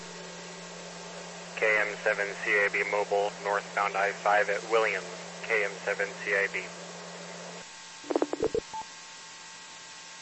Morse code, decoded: none

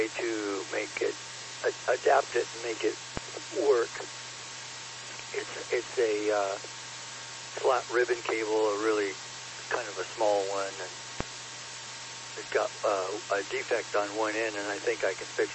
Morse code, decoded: none

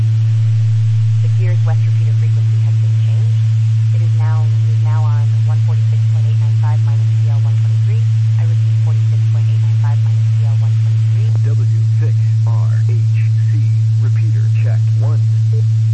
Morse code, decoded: none

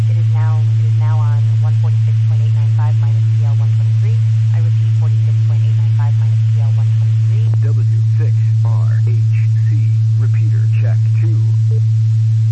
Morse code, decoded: none

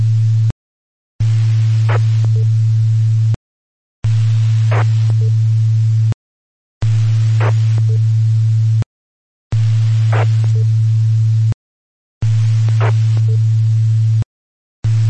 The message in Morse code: none

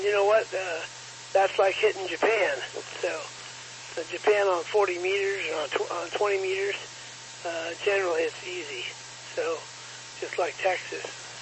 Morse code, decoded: none

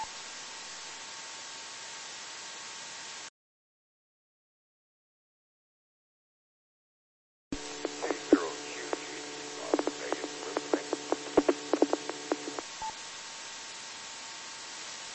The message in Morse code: none

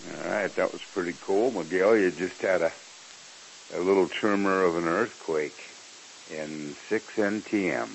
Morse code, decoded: none

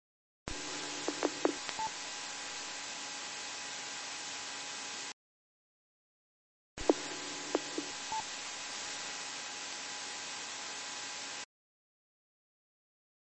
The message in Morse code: none

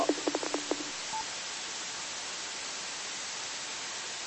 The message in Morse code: none

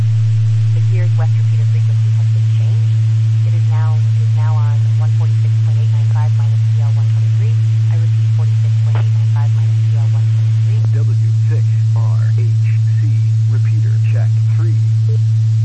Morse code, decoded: none

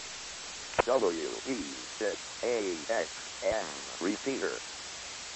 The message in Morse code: none

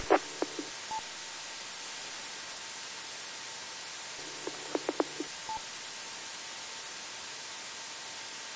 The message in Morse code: none

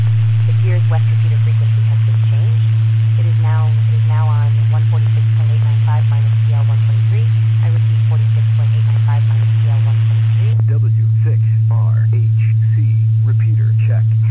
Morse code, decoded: none

